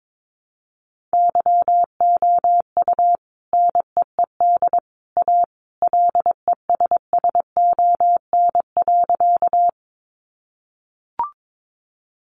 Morse code, heard as a keyed tone, about 22 wpm, 700 Hz. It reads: YOU NEED A LESSON.
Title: Sentences from Top 1200 Words 22wpm